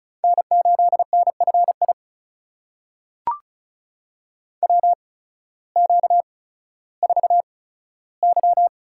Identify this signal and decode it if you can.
Morse code, heard as a keyed tone, about 36 wpm, spaced as for 16 wpm, 700 Hz. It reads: N8NFI WQ4Y
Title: USA Call Signs - 8x Character Spacing - 35wpm